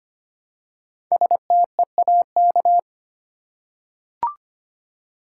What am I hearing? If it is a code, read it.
Morse code, heard as a keyed tone, about 25 wpm, 700 Hz. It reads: STEAK
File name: practice / 25wpm / Easy Words with Hints - 25wpm